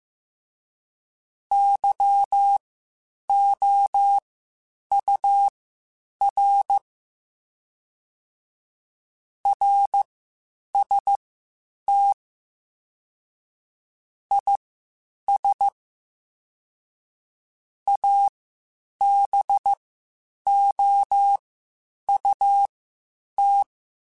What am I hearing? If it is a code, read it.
Morse code, heard as a keyed tone, about 15 wpm, spaced as for 8 wpm, 800 Hz. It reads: YOUR RST IS ABOUT
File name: (spaced) / QSOs Spaced-mp3s / WD0FBT 1 @13 sp